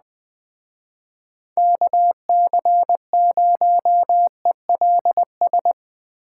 Morse code, heard as a keyed tone, about 20 wpm, 700 Hz. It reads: KC0ELS